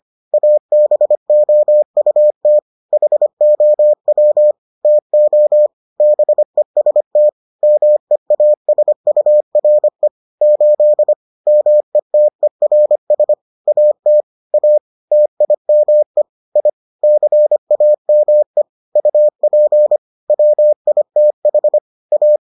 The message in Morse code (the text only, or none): ABOUT HOW TO BEST MEASURE 8 METERS AT A TIME I CAME UP WITH A